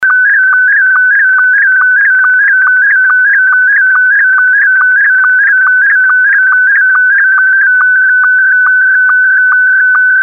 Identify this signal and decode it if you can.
SSTV (slow-scan television), Scottie 1.